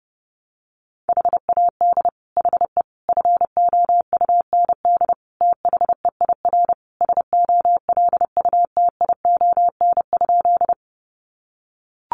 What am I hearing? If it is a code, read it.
Morse code, heard as a keyed tone, about 30 wpm, 700 Hz. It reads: HAD HE FOUND THEIR SOLUTION?
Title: Sentences from Top 1300 Words 30wpm